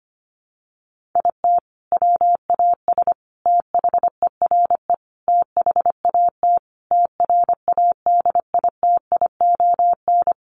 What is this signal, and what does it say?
Morse code, heard as a keyed tone, about 25 wpm, 700 Hz. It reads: IT WAS THERE THAT TRADITION
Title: Sentences from Top 1500 Words 25wpm